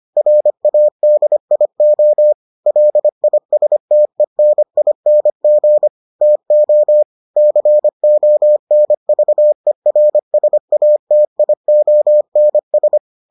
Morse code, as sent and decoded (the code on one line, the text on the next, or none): .-. .- -.. .. --- / .-.. .. ... - . -. .. -. --. / - --- / -.-. --- -. ...- . .-. ... .- - .. --- -. ...
RADIO LISTENING TO CONVERSATIONS